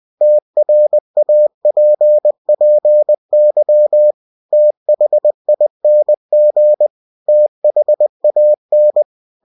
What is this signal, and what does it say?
Morse code, heard as a keyed tone, about 20 wpm, 600 Hz. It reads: TRAPPY THING THAN
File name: cw-books / winnie-the-pooh / winnie-20wpm-page-0037